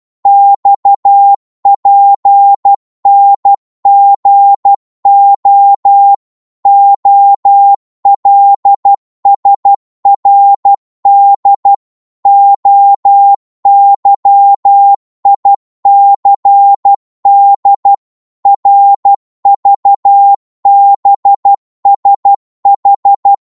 Morse code, heard as a keyed tone, about 12 wpm, 800 Hz. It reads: XPNGO OLSRD OYICD RVBSH
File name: Morse / ltr-12wpm